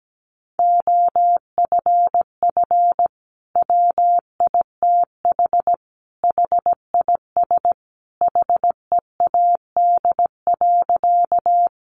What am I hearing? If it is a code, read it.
Morse code, heard as a keyed tone, about 17 wpm, 700 Hz. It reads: OFF WITH HIS HEAD.
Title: Sentences from Top 300 Words 17wpm